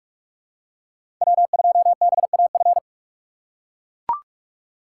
Morse code, heard as a keyed tone, about 43 wpm, 700 Hz. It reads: W2BAF
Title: USA Call Signs - 2x Character Spacing - 45wpm